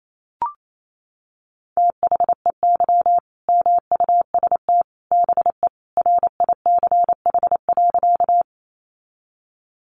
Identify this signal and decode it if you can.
Morse code, heard as a keyed tone, about 28 wpm, 700 Hz. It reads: THEY MUST BE RICH.
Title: Sentences from Top 1100 Words 28wpm